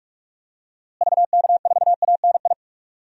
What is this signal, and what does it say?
Morse code, heard as a keyed tone, about 46 wpm, 700 Hz. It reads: VK4ANI
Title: DXCC-100 Call Signs - 6 Characters - Repeated 2x - 45wpm